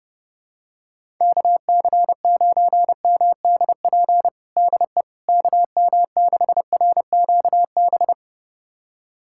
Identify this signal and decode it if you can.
Morse code, heard as a keyed tone, about 30 wpm, 700 Hz. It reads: KC9MDP DE KM6RQB
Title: USA Callsign DE Callsign - 6 Characters - 30wpm